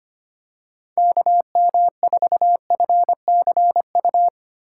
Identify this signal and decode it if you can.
Morse code, heard as a keyed tone, about 25 wpm, 700 Hz. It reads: KM4FCU
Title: USA Call Signs - 6 Characters - 2x Character Spacing - 25wpm